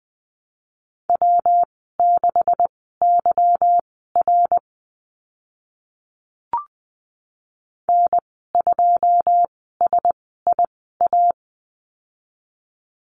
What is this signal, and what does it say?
Morse code, heard as a keyed tone, about 20 wpm, spaced as for 15 wpm, 700 Hz. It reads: W6YR N2SIA
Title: 161 Lesson 26g - Letter g - Call Signs 20wpm